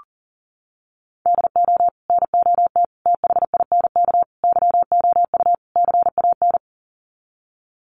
Morse code, heard as a keyed tone, about 37 wpm, 700 Hz. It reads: DO NOT THINK YOU CAN
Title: Sets of 5 Words 40wpm